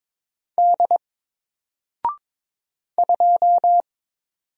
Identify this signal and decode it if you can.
Morse code, heard as a keyed tone, about 22 wpm, 700 Hz. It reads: D 2
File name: Single-Letter-Number-22wpm